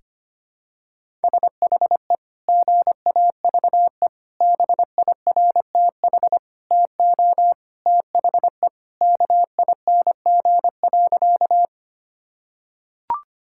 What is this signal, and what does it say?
Morse code, heard as a keyed tone, about 25 wpm, 700 Hz. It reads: SHE GAVE BIRTH TO THE KING.
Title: Sentences from Top 1000 Words - Review Part 5 - 25wpm